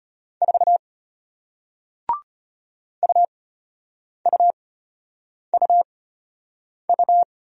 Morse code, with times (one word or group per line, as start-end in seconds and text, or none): 0.4-0.8 4
3.0-7.2 UUUU